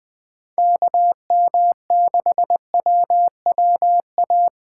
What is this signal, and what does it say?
Morse code, heard as a keyed tone, about 19 wpm, 700 Hz. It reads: KM6WWA
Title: Speed Racing - USA 2x3 Call Signs 17wpm